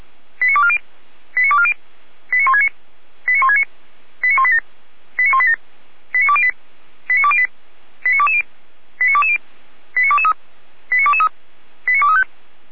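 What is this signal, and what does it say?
Five-tone selective call (ZVEI), heard as quick runs of five tones, tones of about 70 ms. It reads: 89240 89240 89169 89169 89187 89187 89208 89208 89220 89220 89223 89223 89235
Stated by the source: ZVEI five-tone selective call, code 89230, received by SDR, from Feuerwehr Orlamünde.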